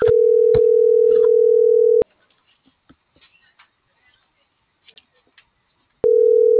Ringback tone (North America), 440 + 480 Hz.